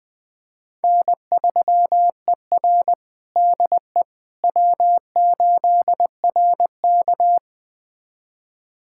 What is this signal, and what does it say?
Morse code, heard as a keyed tone, about 20 wpm, 700 Hz. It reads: N3ER DE W8RK
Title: USA Callsign DE Callsign - 4 Characters - 20wpm